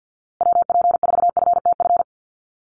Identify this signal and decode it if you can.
Morse code, heard as a keyed tone, about 50 wpm, 700 Hz. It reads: WP4FTF
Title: USA Call Signs Repeated 3x 50wpm